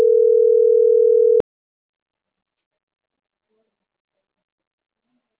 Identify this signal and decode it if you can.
Ringback tone (North America), 440 + 480 Hz.